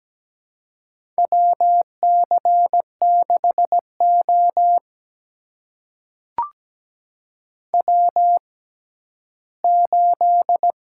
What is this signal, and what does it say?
Morse code, heard as a keyed tone, about 17 wpm, spaced as for 11 wpm, 700 Hz. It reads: WC6O W8
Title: USA Call Signs - 4 Characters - 6x Character Spacing - 17wpm